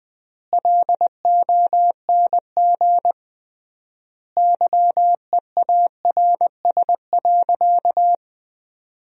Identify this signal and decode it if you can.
Morse code, heard as a keyed tone, about 20 wpm, 700 Hz. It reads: LONG YEARS.
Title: 3x Word-Spacing - Sentences from Top 200 Words - 20wpm